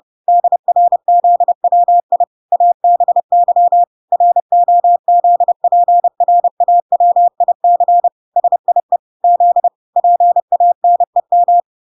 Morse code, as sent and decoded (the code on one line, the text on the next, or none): -.. .-. --.. .-- .. / .- -... -.-- / .-. --- --.. .--. .-. .- .-- .. -.-. / ... .. / --.. / .--. .- -. --
DRZWI ABY ROZPRAWIC SIE Z PANEM